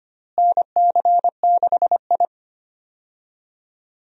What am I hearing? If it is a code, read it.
Morse code, heard as a keyed tone, about 25 wpm, 700 Hz. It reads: NC6I
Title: USA Call Signs - 4 Characters - Repeated 2x - 25wpm